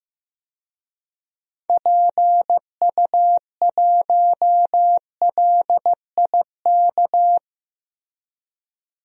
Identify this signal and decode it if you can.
Morse code, heard as a keyed tone, about 15 wpm, 700 Hz. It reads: PU1LIK
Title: DXCC-100 Call Signs - 6 Characters - 15wpm